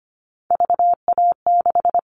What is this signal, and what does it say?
Morse code, heard as a keyed tone, about 25 wpm, 700 Hz. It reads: VA6
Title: Callsign Prefixes for Top 50 DXCC - 25wpm